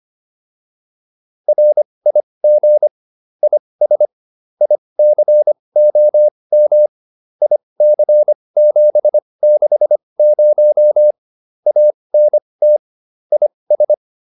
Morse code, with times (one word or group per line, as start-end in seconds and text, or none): 1.5-2.9 RIG
3.4-4.1 IS
4.6-6.9 ICOM
7.4-11.1 IC760
11.7-12.8 ANT
13.3-13.9 IS